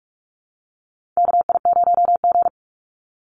Morse code, heard as a keyed tone, about 46 wpm, 700 Hz. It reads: KI0G
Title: NA SOTA Chaser Callsigns 45wpm